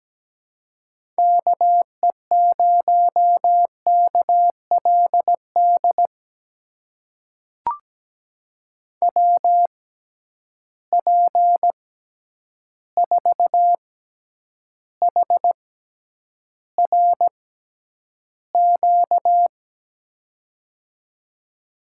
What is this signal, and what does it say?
Morse code, heard as a keyed tone, about 17 wpm, spaced as for 9 wpm, 700 Hz. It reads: KE0KLD WP4HRQ
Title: USA Call Signs - 6 Characters - 6x Character Spacing - 17wpm